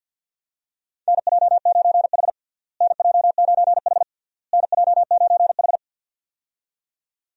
Morse code, heard as a keyed tone, about 47 wpm, 700 Hz. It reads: NJ9H NJ9H NJ9H
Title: USA Call Signs - 4 Characters - Repeated 3x - 50wpm